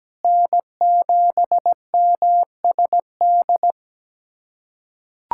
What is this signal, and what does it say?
Morse code, heard as a keyed tone, about 17 wpm, 700 Hz. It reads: N7MSD
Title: USA Callsign DE Callsign 17wpm